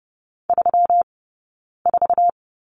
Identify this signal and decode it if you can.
Morse code, heard as a keyed tone, about 30 wpm, spaced as for 9 wpm, 700 Hz. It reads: 34
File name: Lesson 20d - Number 3 - Two Characters 30wpm